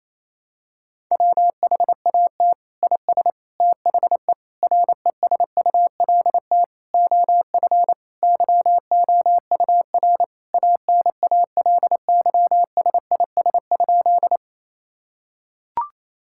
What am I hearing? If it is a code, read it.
Morse code, heard as a keyed tone, about 28 wpm, 700 Hz. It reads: WHAT IS THE RESULT OF YOUR ANALYSIS?
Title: Sentences from Top 1000 Words - Review Part 5 - 28wpm